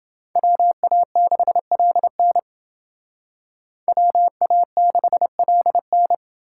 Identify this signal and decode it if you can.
Morse code, heard as a keyed tone, about 29 wpm, 700 Hz. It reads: WA6LN WA6LN